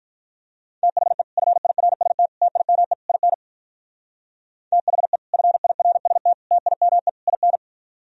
Morse code, heard as a keyed tone, about 52 wpm, 700 Hz. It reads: THE FIRST TIME IN THE FIRST TIME IN